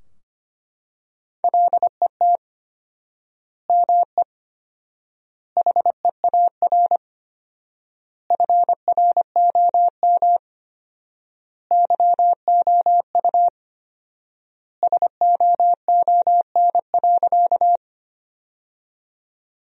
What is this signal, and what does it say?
Morse code, heard as a keyed tone, about 25 wpm, 700 Hz. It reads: LET ME HEAR FROM YOU SOON.
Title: 4x Word-Spacing - Sentences from Top 400 Words - 25wpm